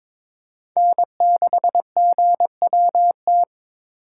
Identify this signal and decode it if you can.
Morse code, heard as a keyed tone, about 22 wpm, 700 Hz. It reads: N6GWT
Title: USA Call Signs - 5 Characters - 8x Character Spacing - 22wpm